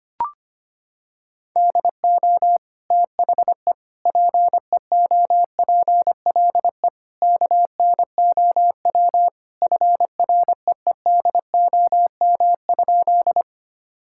Morse code, heard as a keyed tone, about 25 wpm, 700 Hz. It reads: DO THE PEOPLE KNOW FREEDOM?
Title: Sentences from Top 1000 Words - Review Part 5 - 25wpm